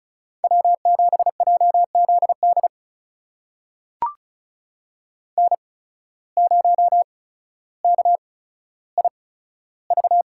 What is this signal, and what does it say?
Morse code, heard as a keyed tone, about 36 wpm, spaced as for 15 wpm, 700 Hz. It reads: W7JZD N0KIV
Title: USA Call Signs - 5 Characters - 8x Character Spacing - 35wpm